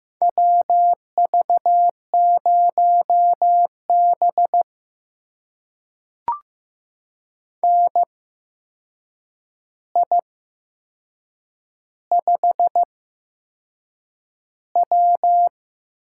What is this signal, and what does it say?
Morse code, heard as a keyed tone, about 15 wpm, spaced as for 6 wpm, 700 Hz. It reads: WV0B NI5W
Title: USA Call Signs - 8x Character Spacing - 15wpm